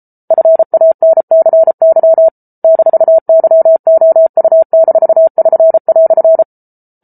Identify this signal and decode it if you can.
Morse code, heard as a keyed tone, about 34 wpm, 650 Hz. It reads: FANCY -YOU-<SN>"